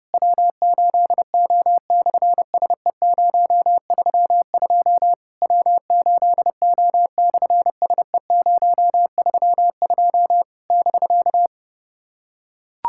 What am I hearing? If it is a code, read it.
Morse code, heard as a keyed tone, about 28 wpm, 700 Hz. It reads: W8O/SE032 W8O/SE032 <BK>
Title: SOTA QSO Practice 30wpm